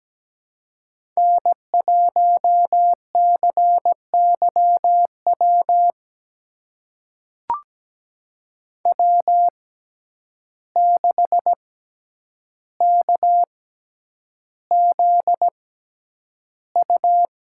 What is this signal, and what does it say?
Morse code, heard as a keyed tone, about 17 wpm, spaced as for 9 wpm, 700 Hz. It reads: N1CYW W6KZU
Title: USA Call Signs - 5 Characters - 6x Character Spacing - 17wpm